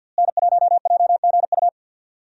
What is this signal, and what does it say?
Morse code, heard as a keyed tone, about 46 wpm, 700 Hz. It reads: N1JGU
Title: USA Call Signs - 5 Characters - Repeated 2x - 50wpm